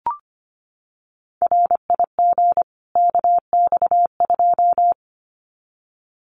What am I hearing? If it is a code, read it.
Morse code, heard as a keyed tone, about 25 wpm, 700 Hz. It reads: RIG KX2
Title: Common QSO Phrases 25wpm